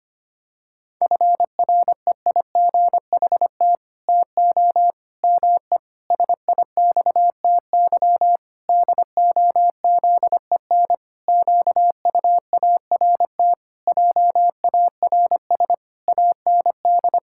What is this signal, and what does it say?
Morse code, heard as a keyed tone, about 25 wpm, 700 Hz. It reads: FREIGHT TO ME SIXTY DOZEN QUART JARS AND
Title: Pangram Challenge v2 - 25wpm